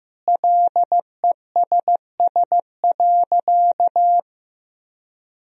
Morse code, heard as a keyed tone, about 15 wpm, 700 Hz. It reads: LESS.
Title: Sentences from Top 1000 Words - Review Part 3 - 15wpm